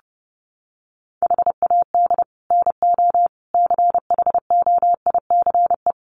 Morse code, heard as a keyed tone, about 30 wpm, 700 Hz. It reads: HAD NO CHOICE
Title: Sets of 5 Words 30wpm